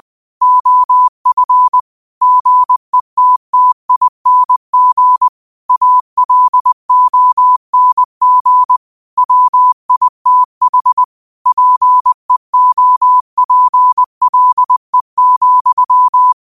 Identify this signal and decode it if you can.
Morse code, heard as a keyed tone, about 20 wpm, 1000 Hz. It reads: OF GETTING ALONG WITH PEOPLE,
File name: qotdfiles / 20wpm